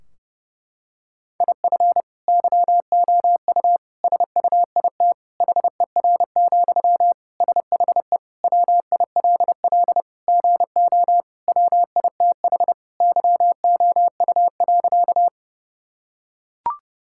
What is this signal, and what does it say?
Morse code, heard as a keyed tone, about 30 wpm, 700 Hz. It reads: IF YOU SUIT HER, SHE WILL GO WITH YOU.
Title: Sentences from Top 1600 Words 30wpm